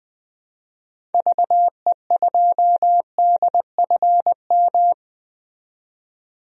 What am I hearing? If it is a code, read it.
Morse code, heard as a keyed tone, about 20 wpm, 700 Hz. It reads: VE2DFM
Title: DXCC-100 Call Signs - 6 Characters - 20wpm